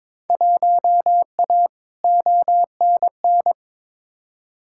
Morse code, heard as a keyed tone, about 22 wpm, 700 Hz. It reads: 1A ONN
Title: Field Day Exchanges Encore - 22wpm